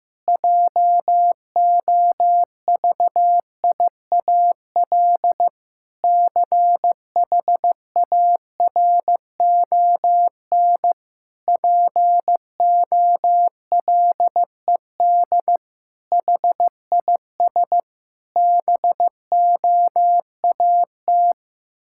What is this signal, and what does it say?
Morse code, heard as a keyed tone, about 15 wpm, 700 Hz. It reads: JOVIAL CHARON POLED HIS BOAT
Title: Pangram Challenge v2 - 15wpm